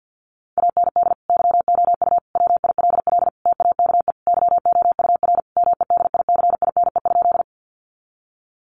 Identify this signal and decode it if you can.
Morse code, heard as a keyed tone, about 50 wpm, 700 Hz. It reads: AND YOU WILL TAKE YOUR MEDICINE?
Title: Sentences from Top 900 Words 50wpm